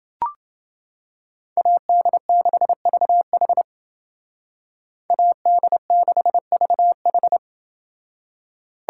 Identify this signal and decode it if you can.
Morse code, heard as a keyed tone, about 29 wpm, 700 Hz. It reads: AD6VH AD6VH